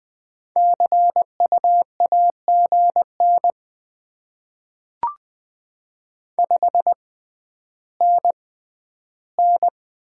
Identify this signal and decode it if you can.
Morse code, heard as a keyed tone, about 20 wpm, spaced as for 11 wpm, 700 Hz. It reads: CU AGN 5NN
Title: Top 20 QSO Elements - 6x Character Spacing - 20wpm